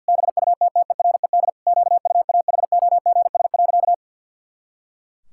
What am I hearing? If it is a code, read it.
Morse code, heard as a keyed tone, about 50 wpm, 700 Hz. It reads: BUTTERED QUAHOGS.
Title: Pangram Challenge v2 - 50wpm